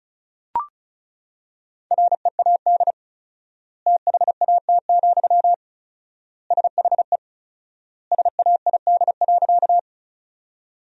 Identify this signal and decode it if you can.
Morse code, heard as a keyed tone, about 34 wpm, 700 Hz. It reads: READ THAT, SHE SAID.
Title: 4x Word-Spacing - Sentences from Top 200 Words - 35wpm